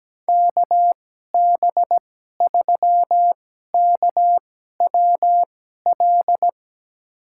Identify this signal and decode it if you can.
Morse code, heard as a keyed tone, about 17 wpm, spaced as for 12 wpm, 700 Hz. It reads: KB3KWL
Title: USA Call Signs - 2x Character Spacing - 17wpm